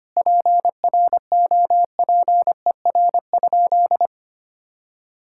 Morse code, heard as a keyed tone, about 25 wpm, 700 Hz. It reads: PROPER?
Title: Sentences from Top 1000 Words - Review Part 5 - 25wpm